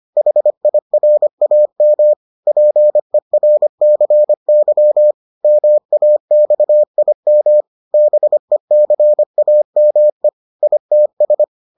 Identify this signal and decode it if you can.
Morse code, heard as a keyed tone, about 25 wpm, 600 Hz. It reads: HIRAM PERCY MAXIM BECAME ITS